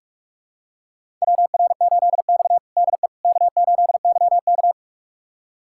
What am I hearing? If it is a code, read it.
Morse code, heard as a keyed tone, about 41 wpm, 700 Hz. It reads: WR8X DE K8YK
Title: USA Callsign DE Callsign - 4 Characters - 45wpm